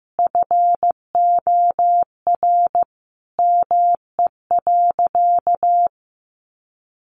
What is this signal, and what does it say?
Morse code, heard as a keyed tone, about 15 wpm, 700 Hz. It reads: FOR ME.
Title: Sentences from Top 1500 Words 15wpm